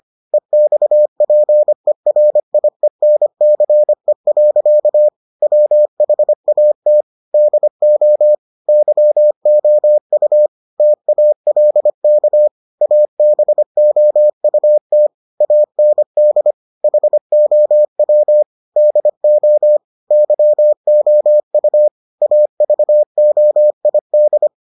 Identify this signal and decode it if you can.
Morse code, heard as a keyed tone, about 25 wpm, 600 Hz. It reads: EXPERIENCE. WHAT DO YOU TALK ABOUT AND HOW DO YOU AVOID